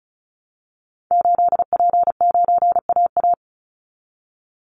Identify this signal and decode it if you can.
Morse code, heard as a keyed tone, about 34 wpm, 700 Hz. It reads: 8P9AA
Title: Call Signs Repeated 3x - 35wpm